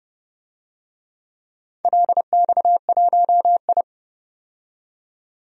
Morse code, heard as a keyed tone, about 31 wpm, 700 Hz. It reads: LX1I
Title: DXCC-100 Call Signs - 4 Characters - 30wpm